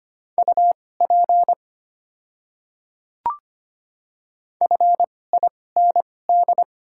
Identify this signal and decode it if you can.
Morse code, heard as a keyed tone, about 25 wpm, spaced as for 18 wpm, 690 Hz. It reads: UP FIND
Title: 2x Character Spacing - Top 100 Words - 25wpm